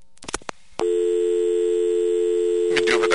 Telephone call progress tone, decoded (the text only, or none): dial tone